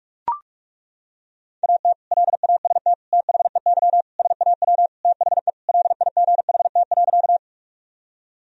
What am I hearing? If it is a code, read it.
Morse code, heard as a keyed tone, about 41 wpm, 700 Hz. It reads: AT LAST THEY SAW THE LIGHT.
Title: Sentences from Top 1000 Words - Review Part 1 - 45wpm